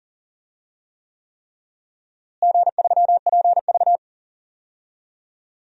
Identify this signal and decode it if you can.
Morse code, heard as a keyed tone, about 37 wpm, 700 Hz. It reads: G3PV
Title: DXCC-100 Call Signs - 4 Characters - Repeated 3x - 40wpm